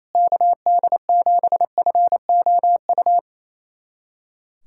Morse code, heard as a keyed tone, about 27 wpm, 700 Hz. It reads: KD7FOU
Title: USA Callsign DE Callsign 28wpm